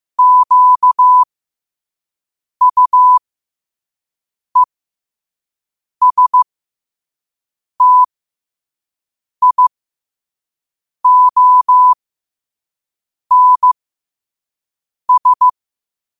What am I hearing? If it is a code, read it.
Morse code, heard as a keyed tone, about 15 wpm, spaced as for 5 wpm, 1000 Hz. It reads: QUESTIONS
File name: qotdfiles / 5wpm